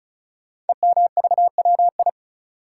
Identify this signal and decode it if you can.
Morse code, heard as a keyed tone, about 36 wpm, 700 Hz. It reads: EMVWI
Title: Code Groups - 5 Letters - 35wpm